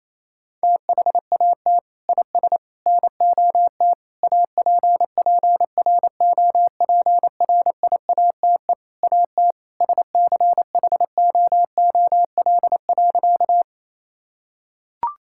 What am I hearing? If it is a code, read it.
Morse code, heard as a keyed tone, about 28 wpm, 700 Hz. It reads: THAT IS NOT APPROPRIATE AT SCHOOL.
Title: Sentences from Top 1000 Words - Review Part 5 - 28wpm